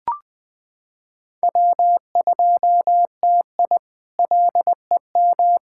Morse code, heard as a keyed tone, about 20 wpm, 700 Hz. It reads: W2TI LEM
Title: K1USN SST QSO Practice - 20wpm